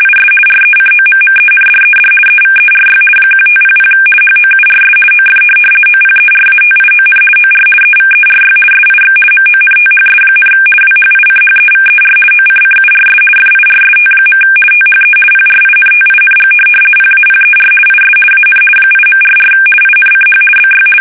RTTY, about 50 baud. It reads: IS RECOMMENDED EVERY 1500 HOURS OF OPERATION OR EVERY SIX MONTHS, WHICHEVER OCCURS FIRST. THE PRINTING PAPER IS AN 8.44 INCH BY
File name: fsk850-50B / teletype-corp